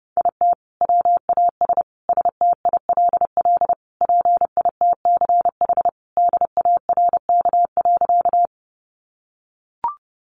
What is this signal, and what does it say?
Morse code, heard as a keyed tone, about 30 wpm, 700 Hz. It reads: IT WAS STILL PITCH DARK.
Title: Sentences from Top 1300 Words 30wpm